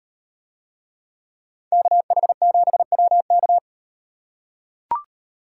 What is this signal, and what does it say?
Morse code, heard as a keyed tone, about 36 wpm, 700 Hz. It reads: KH7WK